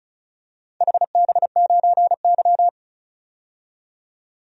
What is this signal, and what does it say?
Morse code, heard as a keyed tone, about 34 wpm, 700 Hz. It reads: HB9Y
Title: DXCC-100 Call Signs - 4 Characters - 35wpm